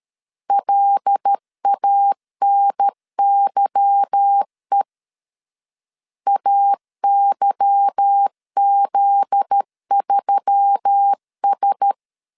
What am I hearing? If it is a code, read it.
Morse code, heard as a keyed tone, about 13 wpm, 800 Hz. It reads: LANYE AYZ3S
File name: LEZIONI / 8 YJ45 / 8 Y 12wpm 2228